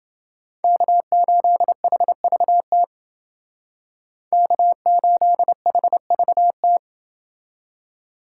Morse code, heard as a keyed tone, about 28 wpm, 700 Hz. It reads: K8HVT K8HVT